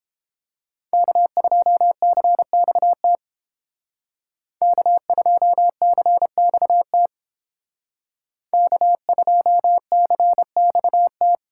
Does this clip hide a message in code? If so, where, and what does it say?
Yes: Morse code from 0.9 to 11.4 s, heard as a keyed tone, about 30 wpm, 700 Hz. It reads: K2CXT K2CXT K2CXT